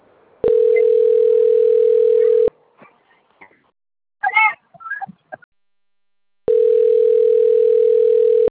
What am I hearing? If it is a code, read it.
Ringback tone (North America), 440 + 480 Hz.